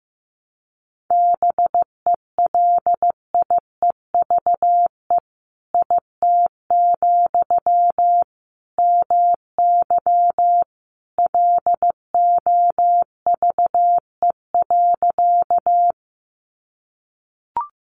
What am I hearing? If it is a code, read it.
Morse code, heard as a keyed tone, about 15 wpm, 700 Hz. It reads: BELIEVE IT, MY LOVE.
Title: Sentences from Top 1000 Words 15wpm